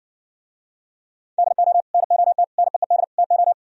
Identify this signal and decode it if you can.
Morse code, heard as a keyed tone, about 59 wpm, 700 Hz. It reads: DO NOT NEED TO